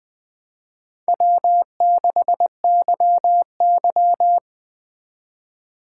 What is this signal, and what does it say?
Morse code, heard as a keyed tone, about 24 wpm, 700 Hz. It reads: W6YY
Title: Speed Racing - USA Call Signs 20wpm